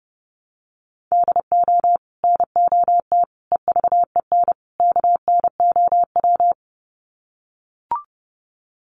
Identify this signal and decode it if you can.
Morse code, heard as a keyed tone, about 30 wpm, 700 Hz. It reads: DO NOT EVEN KNOW
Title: Sets of 4 Words 30wpm